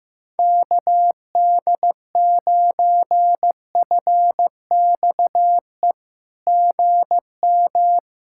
Morse code, heard as a keyed tone, about 15 wpm, 700 Hz. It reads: KD9FXE GM
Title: SOTA QSO Practice 15wpm